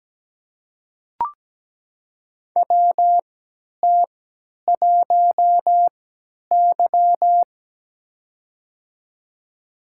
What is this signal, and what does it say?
Morse code, heard as a keyed tone, about 17 wpm, spaced as for 10 wpm, 700 Hz. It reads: WT1Y
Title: USA Call Signs - 3x Character Spacing - 17wpm